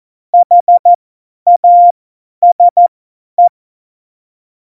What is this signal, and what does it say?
Morse code, heard as a keyed tone, about 14 wpm, spaced as for 10 wpm, 700 Hz. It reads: HASE
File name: Morsekurs / Witze / Witz 03. November 2020, 50 F70